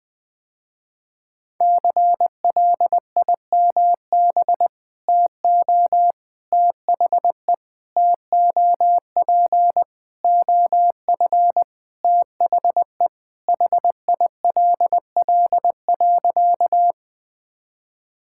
Morse code, heard as a keyed tone, about 20 wpm, 700 Hz. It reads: CLIMB TO THE TOP OF THE HILL.
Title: Sentences with 4-Word Phrases - Spoken Beginning - 20wpm